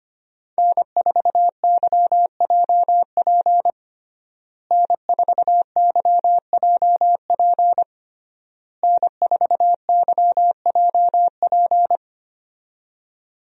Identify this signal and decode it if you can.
Morse code, heard as a keyed tone, about 25 wpm, 700 Hz. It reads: N4YJP N4YJP N4YJP
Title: USA Call Signs - 5 Characters - Repeated 3x - 25wpm